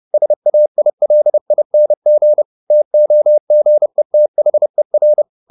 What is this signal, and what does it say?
Morse code, heard as a keyed tone, about 30 wpm, 600 Hz. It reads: SAILING TOGETHER